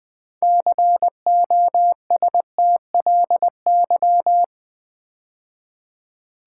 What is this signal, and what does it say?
Morse code, heard as a keyed tone, about 20 wpm, 700 Hz. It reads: COSTLY
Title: 6 Letter Words - First 3 Letters Given - 20wpm